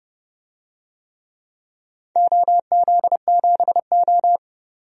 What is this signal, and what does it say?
Morse code, heard as a keyed tone, about 30 wpm, 700 Hz. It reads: OZ7O
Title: DXCC-100 Call Signs - 4 Characters - 30wpm